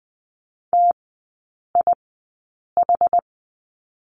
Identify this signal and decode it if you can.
Morse code, heard as a keyed tone, about 20 wpm, spaced as for 8 wpm, 700 Hz. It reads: TIH
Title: 060 Lesson 11e - Letter h - Three Characters 20wpm